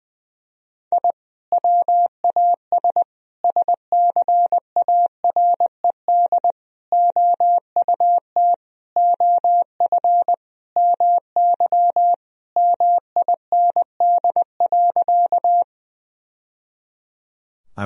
Morse code, heard as a keyed tone, about 20 wpm, 700 Hz. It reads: I WAS SCARED OUT OF MY MIND.
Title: Sentences from Top 1000 Words - Review Part 4 - 20wpm